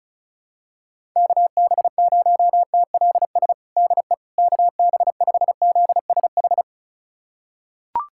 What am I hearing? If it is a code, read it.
Morse code, heard as a keyed tone, about 36 wpm, 700 Hz. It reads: KB0TLS DE KB5ZSH
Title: USA Callsign DE Callsign - 6 Characters - 35wpm